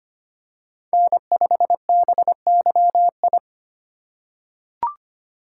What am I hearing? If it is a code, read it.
Morse code, heard as a keyed tone, about 25 wpm, 700 Hz. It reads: N5BYI
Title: USA Call Signs - 3x Character Spacing - 25wpm